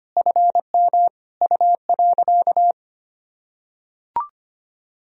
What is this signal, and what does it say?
Morse code, heard as a keyed tone, about 25 wpm, 700 Hz. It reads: FM U.
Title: Sentences from Top 100 Words and Abbreviations 25wpm